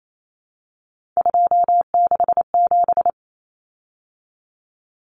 Morse code, heard as a keyed tone, about 28 wpm, 700 Hz. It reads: 267
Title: Common Cut Numbers - 3-Digit 28wpm